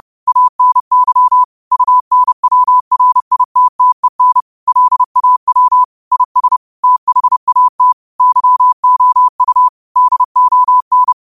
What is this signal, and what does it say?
Morse code, heard as a keyed tone, about 30 wpm, 1000 Hz. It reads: ANY UNWRITTEN LAW IS THAT YOU DON
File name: qotdfiles / 30wpm